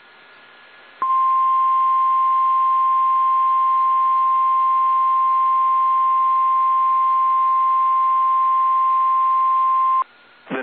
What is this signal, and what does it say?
Emergency Alert System (SAME) tones (the 1050 Hz attention tone). It is heard from 1.1 to 9.6 s.
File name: eas / KSDZKDJL / KSDZKDJL